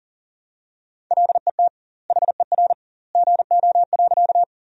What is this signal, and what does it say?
Morse code, heard as a keyed tone, about 40 wpm, 700 Hz. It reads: LET HER GO.
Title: 2x Word-Spacing - Sentences from Top 300 Words - 40wpm